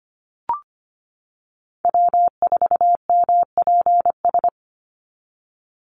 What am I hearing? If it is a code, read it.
Morse code, heard as a keyed tone, about 25 wpm, 700 Hz. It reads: W4MPS
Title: NA SOTA Chaser Callsigns 25wpm